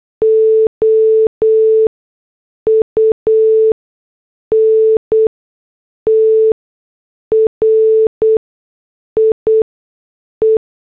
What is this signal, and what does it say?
Morse code, heard as a keyed tone, about 8 wpm, spaced as for 6 wpm, 440 Hz. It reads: OUNTRIE
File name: morse / headlines